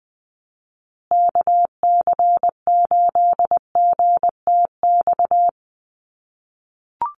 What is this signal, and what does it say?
Morse code, heard as a keyed tone, about 20 wpm, 700 Hz. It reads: KC8GTX
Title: USA Call Signs Repeated 2x 20wpm